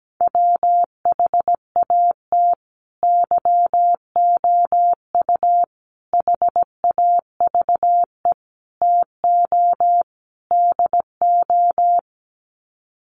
Morse code, heard as a keyed tone, about 17 wpm, 700 Hz. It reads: WHAT YOU HAVE TO DO
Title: Sets of 5 Words Encore 17wpm